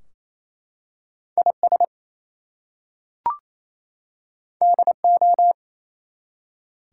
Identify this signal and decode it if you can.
Morse code, heard as a keyed tone, about 27 wpm, 700 Hz. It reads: IS DO
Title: Two Letter Words - 28wpm